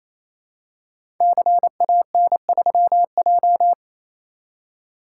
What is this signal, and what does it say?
Morse code, heard as a keyed tone, about 27 wpm, 700 Hz. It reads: CAN3J
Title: Code Groups - 5 Letter:Numbers - 28wpm